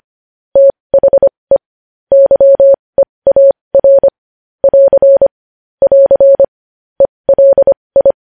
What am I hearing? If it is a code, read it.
Morse code, heard as a keyed tone, about 25 wpm, spaced as for 20 wpm, 550 Hz. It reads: THE YEAR <AR> <AR> ELI
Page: News headlines in morse code at 20 wpm